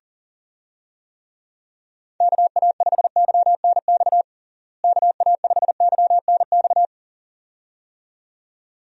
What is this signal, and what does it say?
Morse code, heard as a keyed tone, about 40 wpm, 700 Hz. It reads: KA5YNX KA5YNX
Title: USA Call Signs - 6 Characters - Repeated 2x - 40wpm